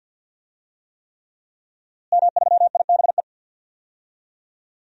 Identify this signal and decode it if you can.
Morse code, heard as a keyed tone, about 52 wpm, 700 Hz. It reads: M3IBE
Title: DXCC-100 Call Signs - 5 Characters - Repeated 3x - 50wpm